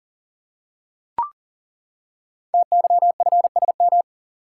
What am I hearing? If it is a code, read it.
Morse code, heard as a keyed tone, about 38 wpm, 690 Hz. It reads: TYFSM
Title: Code Groups - 5 Letters - 40wpm